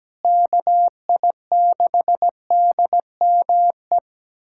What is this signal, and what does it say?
Morse code, heard as a keyed tone, about 17 wpm, 700 Hz. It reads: KI6DME
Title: USA Callsign DE Callsign - 6 Characters - 17wpm